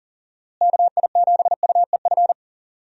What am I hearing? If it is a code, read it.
Morse code, heard as a keyed tone, about 39 wpm, 700 Hz. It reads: KI7UEF
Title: USA Call Signs - 6 Characters - Repeated 3x - 40wpm